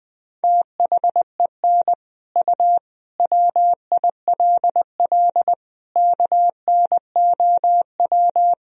Morse code, heard as a keyed tone, about 21 wpm, 700 Hz. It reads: THEN U WILL KNOW
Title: Sentences from Top 400 Words and Abbreviations 20wpm